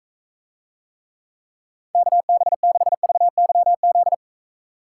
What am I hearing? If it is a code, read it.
Morse code, heard as a keyed tone, about 42 wpm, 700 Hz. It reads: KB6VYZ